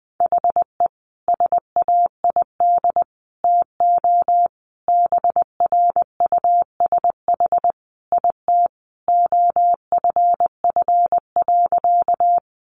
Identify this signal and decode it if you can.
Morse code, heard as a keyed tone, about 20 wpm, 700 Hz. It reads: HE SAID TO BRUSH IT OFF.
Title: Sentences from Top 1100 Words 20wpm